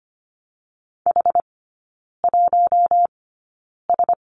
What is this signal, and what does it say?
Morse code, heard as a keyed tone, about 25 wpm, spaced as for 9 wpm, 690 Hz. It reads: H1S
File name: Lesson 14e - Number 2 - Three Characters 25wpm